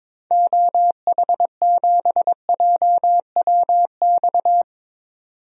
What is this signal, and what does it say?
Morse code, heard as a keyed tone, about 22 wpm, 700 Hz. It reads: OH7JWX
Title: DXCC-100 Call Signs - 6 Characters - 22wpm